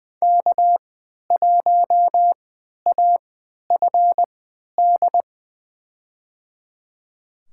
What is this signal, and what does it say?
Morse code, heard as a keyed tone, about 21 wpm, spaced as for 11 wpm, 700 Hz. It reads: K1AFD
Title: USA Call Signs - 3x Character Spacing - 20wpm